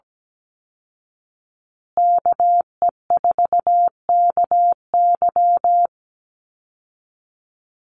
Morse code, heard as a keyed tone, about 17 wpm, 700 Hz. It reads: KE4KY
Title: Call Signs Repeated 2x - 17wpm